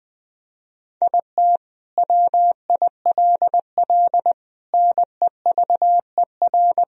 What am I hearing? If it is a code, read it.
Morse code, heard as a keyed tone, about 20 wpm, 700 Hz. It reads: IT WILL NEVER
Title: Sentences from Top 1000 Words - Review Part 2 - 20wpm